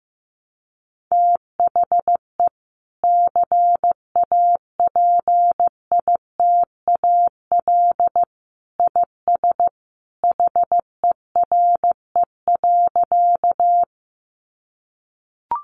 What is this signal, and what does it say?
Morse code, heard as a keyed tone, about 15 wpm, 700 Hz. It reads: THE CAPITAL IS HERE.
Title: Sentences from Top 1000 Words 15wpm